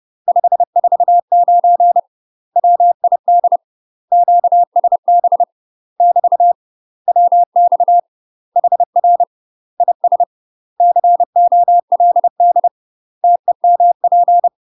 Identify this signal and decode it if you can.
Morse code, heard as a keyed tone, about 30 wpm, 700 Hz. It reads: 549 WID QSB <BT> WX HR IS COLD TEMP
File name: UK MASTER QSOs TRAINING / 30WPM / UK MASTER QSOs 0047